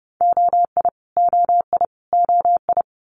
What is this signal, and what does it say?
Morse code, heard as a keyed tone, about 30 wpm, 700 Hz. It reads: OI OI OI
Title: Multi-letter Phonemes Repeated 3x 30wpm